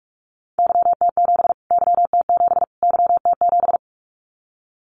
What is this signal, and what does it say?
Morse code, heard as a keyed tone, about 44 wpm, 700 Hz. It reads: YT7 YT7 YT7
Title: Callsign Prefixes for Top 25 DXCC - Repeated 3x - 45wpm